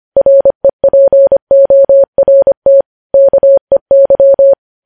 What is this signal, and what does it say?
Morse code, heard as a keyed tone, about 25 wpm, 550 Hz. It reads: REPORT KEY
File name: podcast / News headlines-1x-25x25